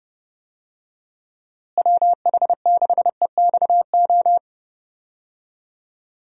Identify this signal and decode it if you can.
Morse code, heard as a keyed tone, about 32 wpm, 700 Hz. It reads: WH6EXO
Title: USA Call Signs - 6 Characters - Repeated 3x - 30wpm